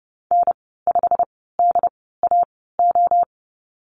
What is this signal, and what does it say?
Morse code, heard as a keyed tone, about 31 wpm, 700 Hz. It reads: N 5 D A O
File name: Lesson 19g - Letter w - Call Signs 30wpm